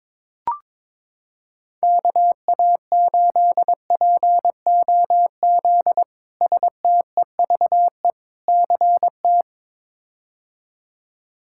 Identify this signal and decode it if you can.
Morse code, heard as a keyed tone, about 22 wpm, 700 Hz. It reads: KA8POZ STEVE CT
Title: K1USN SST QSO Practice - 22wpm